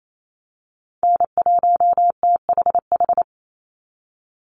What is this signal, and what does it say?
Morse code, heard as a keyed tone, about 28 wpm, 700 Hz. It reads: N1THH
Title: USA Call Signs 28wpm